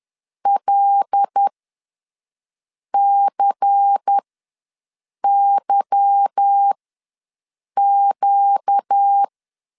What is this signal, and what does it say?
Morse code, heard as a keyed tone, about 10 wpm, spaced as for 5 wpm, 800 Hz. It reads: LCYQ
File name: LEZIONI / 8 YJ45 / 8 YJ4 ws 10wpm 4020